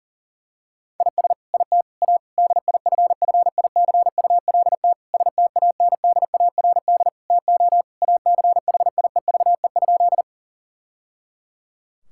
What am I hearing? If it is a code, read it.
Morse code, heard as a keyed tone, about 40 wpm, 700 Hz. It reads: IS IT A DIFFICULT STANDARD TO ACHIEVE?
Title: Sentences from Top 1000 Words - Review Part 6 - 40wpm